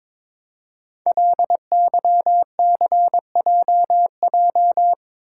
Morse code, heard as a keyed tone, about 22 wpm, 700 Hz. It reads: LYCJJ
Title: Code Groups - 5 Letter:Numbers - 22wpm